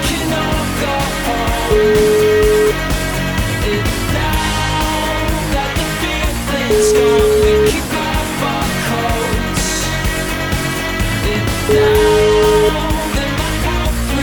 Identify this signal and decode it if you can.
Ringback tone (Europe, 425 Hz), with noise.